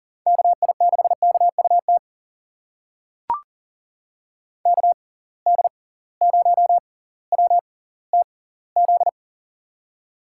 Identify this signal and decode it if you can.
Morse code, heard as a keyed tone, about 40 wpm, spaced as for 20 wpm, 700 Hz. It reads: KI6KUT KD0WTZ
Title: USA Call Signs - 6 Characters - 6x Character Spacing - 40wpm